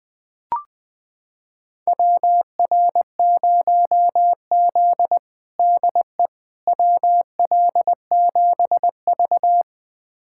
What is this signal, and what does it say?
Morse code, heard as a keyed tone, about 20 wpm, 700 Hz. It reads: WR0Z DE WL7V
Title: USA Callsign DE Callsign - 4 Characters - 20wpm